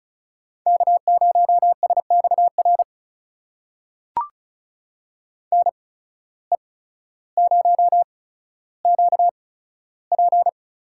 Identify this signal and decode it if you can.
Morse code, heard as a keyed tone, about 34 wpm, spaced as for 15 wpm, 700 Hz. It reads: K0SXR NE0QP
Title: USA Call Signs - 8x Character Spacing - 35wpm